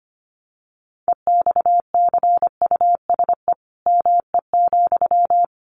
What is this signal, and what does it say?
Morse code, heard as a keyed tone, about 25 wpm, 700 Hz. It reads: EXCUSE ME,
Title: Sentences from Top 700 Words 25wpm